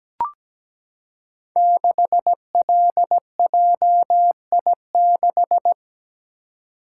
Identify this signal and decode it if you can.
Morse code, heard as a keyed tone, about 17 wpm, 700 Hz. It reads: 6LJI6
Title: Code Groups - 5 Letter:Numbers - 17wpm